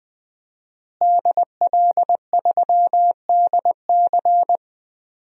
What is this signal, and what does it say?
Morse code, heard as a keyed tone, about 21 wpm, 700 Hz. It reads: DL3DC
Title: DXCC-100 Call Signs - 5 Characters - Repeated 3x - 20wpm